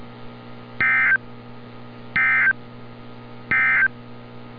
Emergency Alert System (SAME) tones, heard as harsh buzzing bursts. It is heard from 0.7 to 4.0 s.
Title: NOAA Weather Radio - Altoona, PA